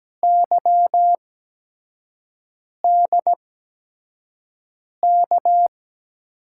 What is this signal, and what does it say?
Morse code, heard as a keyed tone, about 17 wpm, spaced as for 5 wpm, 700 Hz. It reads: YDK
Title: USA Call Signs - 6 Characters - 8x Character Spacing - 17wpm